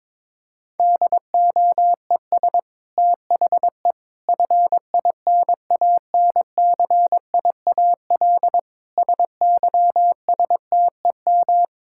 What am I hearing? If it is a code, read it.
Morse code, heard as a keyed tone, about 22 wpm, 700 Hz. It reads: DOES THE FINANCIAL SYSTEM
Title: Sentences from Top 1000 Words - Review Part 5 - 22wpm